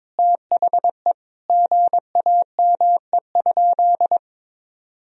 Morse code, heard as a keyed tone, about 22 wpm, 700 Hz. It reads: THE GAME?
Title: Sentences from Top 1000 Words - Review Part 4 - 22wpm